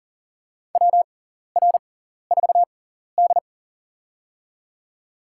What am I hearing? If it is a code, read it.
Morse code, heard as a keyed tone, about 40 wpm, spaced as for 14 wpm, 700 Hz. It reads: WR4D
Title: USA Call Signs - 6x Character Spacing - 40wpm